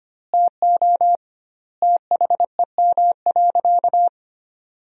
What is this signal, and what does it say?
Morse code, heard as a keyed tone, about 25 wpm, 700 Hz. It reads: TO THEM.
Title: 2x Word-Spacing - Sentences from Top 200 Words - 25wpm